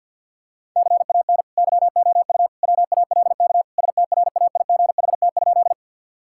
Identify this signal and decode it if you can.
Morse code, heard as a keyed tone, about 50 wpm, 700 Hz. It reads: CAN YOU WALK STRAIGHT?
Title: Sentences from Top 1000 Words - Review Part 6 - 50wpm